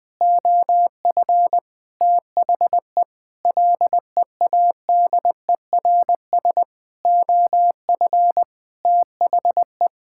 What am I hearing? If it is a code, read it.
Morse code, heard as a keyed tone, about 20 wpm, 700 Hz. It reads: OF THE LEADERS OF THE
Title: Sentences with 6-Word Phrases - 20wpm